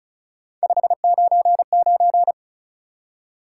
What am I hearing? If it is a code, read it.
Morse code, heard as a keyed tone, about 44 wpm, 700 Hz. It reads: 599